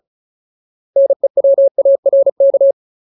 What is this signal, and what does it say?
Morse code, heard as a keyed tone, about 34 wpm, 550 Hz. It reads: NEWARK
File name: Ultimate Word List / 35WPM Ultimate